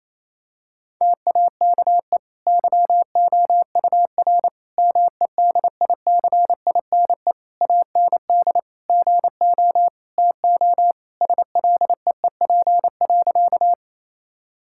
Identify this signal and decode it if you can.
Morse code, heard as a keyed tone, about 28 wpm, 700 Hz. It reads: TAKE YOUR MEDICINE AND GO TO SLEEP.
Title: Sentences from Top 1000 Words - Review Part 6 - 28wpm